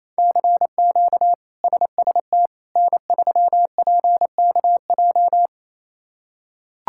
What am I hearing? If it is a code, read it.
Morse code, heard as a keyed tone, about 27 wpm, 700 Hz. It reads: CQ SST N3PKJ
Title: K1USN SST QSO Practice - 28wpm